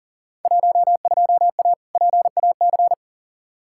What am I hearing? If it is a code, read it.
Morse code, heard as a keyed tone, about 40 wpm, 700 Hz. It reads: 12A PAC
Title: Field Day Exchanges Encore - 40wpm